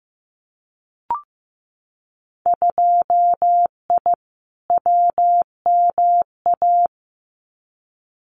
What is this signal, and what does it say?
Morse code, heard as a keyed tone, about 15 wpm, 700 Hz. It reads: 2I WMA
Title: Winter Field Day Exchanges Encore 15wpm